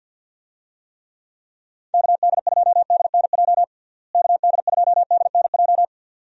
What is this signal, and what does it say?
Morse code, heard as a keyed tone, about 46 wpm, 700 Hz. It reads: KD2DNJ KD2DNJ
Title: USA Call Signs - 6 Characters - Repeated 2x - 50wpm